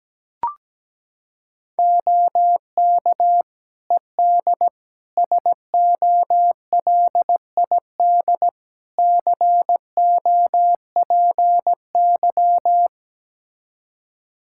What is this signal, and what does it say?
Morse code, heard as a keyed tone, about 17 wpm, 700 Hz. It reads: OK ED SOLID COPY
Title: Practice QSO - Exchange 3 - More Info - 17wpm